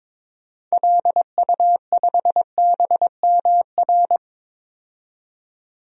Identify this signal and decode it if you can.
Morse code, heard as a keyed tone, about 22 wpm, 700 Hz. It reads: LU5BMR
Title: DXCC-100 Call Signs - 6 Characters - 22wpm